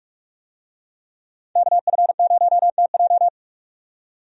Morse code, heard as a keyed tone, about 45 wpm, 700 Hz. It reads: KF0TJ
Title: USA Call Signs - 5 Characters - 45wpm